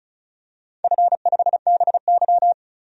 Morse code, heard as a keyed tone, about 36 wpm, 700 Hz. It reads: F5BY
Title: DXCC-100 Call Signs - 4 Characters - 35wpm